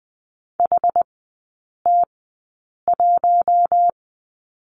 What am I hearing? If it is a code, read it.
Morse code, heard as a keyed tone, about 20 wpm, spaced as for 8 wpm, 700 Hz. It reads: HT1
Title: 060 Lesson 11e - Letter h - Three Characters 20wpm